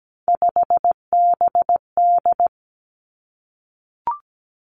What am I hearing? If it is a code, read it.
Morse code, heard as a keyed tone, about 17 wpm, 700 Hz. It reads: 5BD
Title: Three Challenging Letter-Number Combinations 17wpm